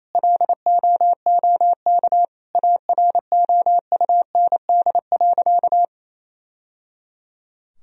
Morse code, heard as a keyed tone, about 28 wpm, 700 Hz. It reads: LOOK AROUND.
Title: Sentences from Top 1000 Words - Review Part 3 - 28wpm